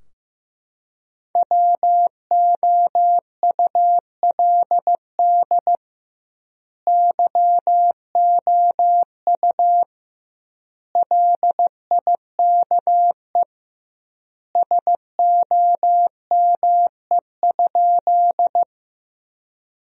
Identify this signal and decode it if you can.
Morse code, heard as a keyed tone, about 16 wpm, 700 Hz. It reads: WOULD YOU LIKE SOME?
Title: 2x Word-Spacing - Sentences from Top 100 Words - 15wpm